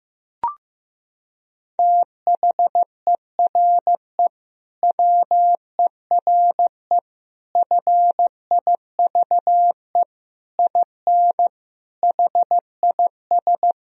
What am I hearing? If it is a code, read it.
Morse code, heard as a keyed tone, about 15 wpm, 700 Hz. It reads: THERE WERE FIVE IN HIS
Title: Sentences from Top 1000 Words - Review Part 2 - 15wpm